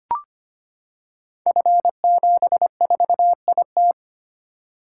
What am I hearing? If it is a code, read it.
Morse code, heard as a keyed tone, about 25 wpm, 700 Hz. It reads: F74IT
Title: Code Groups - 5 Letter:Numbers - 25wpm